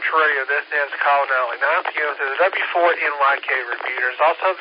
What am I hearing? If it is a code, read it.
Emergency Alert System (SAME) tones, heard as harsh buzzing bursts, with speech. It is heard from 0.3 to 4.1 s.